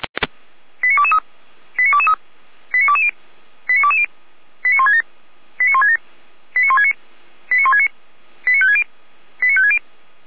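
Five-tone selective call (ZVEI), heard as quick runs of five tones, tones of about 70 ms. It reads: 89223 89223 89220 89220 89167 89167 89169 89169 89560 89560
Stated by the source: ZVEI five-tone selective call received by SDR.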